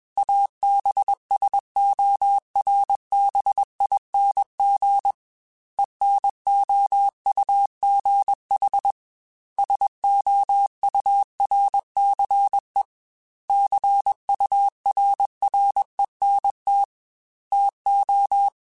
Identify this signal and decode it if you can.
Morse code, heard as a keyed tone, about 21 wpm, 800 Hz. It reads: ABSORBING ENOUGH SOURCE CURRENT TO
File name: Technical Subjects / Technical Subjects-mp3s / XC @20